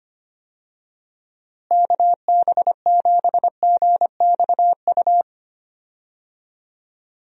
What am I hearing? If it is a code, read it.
Morse code, heard as a keyed tone, about 25 wpm, 700 Hz. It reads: KB7GXU
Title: USA Call Signs - 6 Characters - Repeated 3x - 25wpm